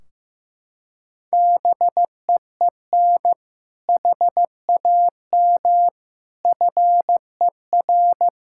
Morse code, heard as a keyed tone, about 15 wpm, 700 Hz. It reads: BEEN HAM FER
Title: Practice QSO - Exchange 3 - More Info - 15wpm